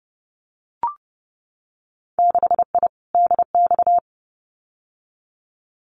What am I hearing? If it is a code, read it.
Morse code, heard as a keyed tone, about 30 wpm, 700 Hz. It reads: 6I DX
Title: Winter Field Day Exchanges Encore 30wpm